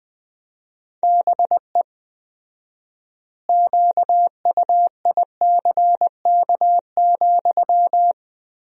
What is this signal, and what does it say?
Morse code, heard as a keyed tone, about 21 wpm, 700 Hz. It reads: BE QUICK,
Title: 4x Word-Spacing - Sentences from Top 500 Words - 20wpm